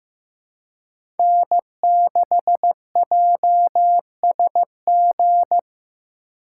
Morse code, heard as a keyed tone, about 15 wpm, 700 Hz. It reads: N6JSG
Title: USA Call Signs - 5 Characters - 4x Character Spacing - 15wpm